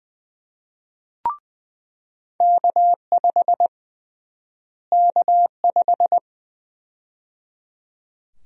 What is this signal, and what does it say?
Morse code, heard as a keyed tone, about 20 wpm, 700 Hz. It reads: K5 K5
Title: USA Call Sign Prefixes - Repeated 2x - 20wpm